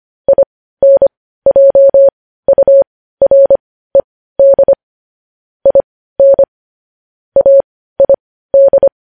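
Morse code, heard as a keyed tone, about 25 wpm, spaced as for 15 wpm, 550 Hz. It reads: INJURED IN AID
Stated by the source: News headlines sent in Morse code at 15 words per minute, preceded by vvv.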